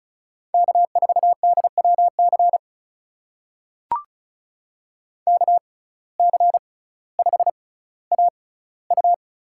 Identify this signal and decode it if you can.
Morse code, heard as a keyed tone, about 34 wpm, spaced as for 18 wpm, 700 Hz. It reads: K4DWC KC5AU
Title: USA Call Signs - 5 Characters - 6x Character Spacing - 35wpm